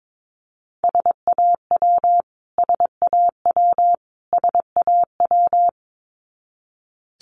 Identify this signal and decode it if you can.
Morse code, heard as a keyed tone, about 22 wpm, 700 Hz. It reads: SAW SAW SAW
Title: Top 500 Words Repeated 22wpm